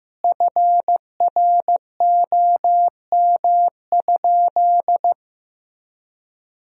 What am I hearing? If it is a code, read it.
Morse code, heard as a keyed tone, about 16 wpm, 700 Hz. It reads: FROM?
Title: 2x Word-Spacing - Sentences from Top 300 Words - 15wpm